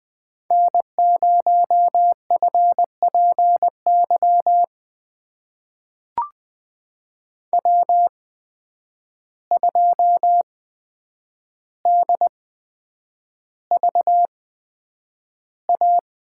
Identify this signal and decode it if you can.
Morse code, heard as a keyed tone, about 20 wpm, spaced as for 9 wpm, 700 Hz. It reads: N0FPY W2DVA
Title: USA Call Signs - 5 Characters - 8x Character Spacing - 20wpm